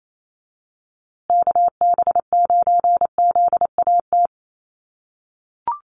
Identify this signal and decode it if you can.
Morse code, heard as a keyed tone, about 28 wpm, 700 Hz. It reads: KB9ZAT
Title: USA Call Signs Repeated 3x 28wpm